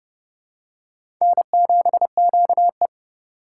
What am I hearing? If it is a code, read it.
Morse code, heard as a keyed tone, about 30 wpm, 700 Hz. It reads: N7QE
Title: USA Call Signs - 4 Characters - 3x Character Spacing - 30wpm